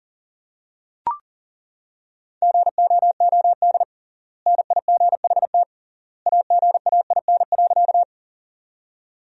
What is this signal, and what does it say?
Morse code, heard as a keyed tone, about 40 wpm, 700 Hz. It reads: GOOD NIGHT AGAIN.
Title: 3x Word-Spacing - Sentences from Top 300 Words - 40wpm